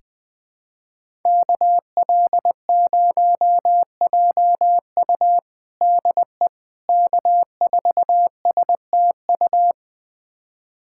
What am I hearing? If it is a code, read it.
Morse code, heard as a keyed tone, about 20 wpm, 700 Hz. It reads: KL0JU DE K4STU
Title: USA Callsign DE Callsign - 5 Characters - 20wpm